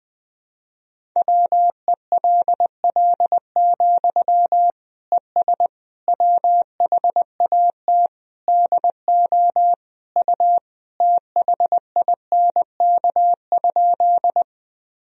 Morse code, heard as a keyed tone, about 21 wpm, 700 Hz. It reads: WELL, ES WHAT DO U THINK?
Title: Sentences from Top 200 Words and Abbreviations 20wpm